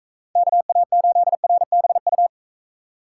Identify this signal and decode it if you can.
Morse code, heard as a keyed tone, about 44 wpm, 700 Hz. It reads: KA8RBU